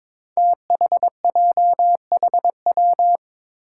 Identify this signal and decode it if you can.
Morse code, heard as a keyed tone, about 22 wpm, 700 Hz. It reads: THJHW
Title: Five Challenging Letter-Number Combinations 22wpm